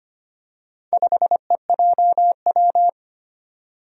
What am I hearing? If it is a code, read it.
Morse code, heard as a keyed tone, about 25 wpm, 700 Hz. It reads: 5EJW
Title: Four Challenging Letter-Number Combinations 25wpm